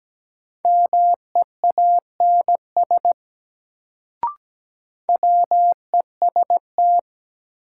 Morse code, heard as a keyed tone, about 17 wpm, 700 Hz. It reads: MEANS WEST
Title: Sets of 5 Words - New Word Drill - 17wpm